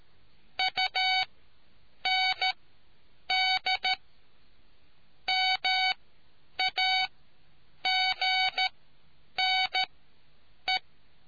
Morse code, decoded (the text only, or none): UND MAGNE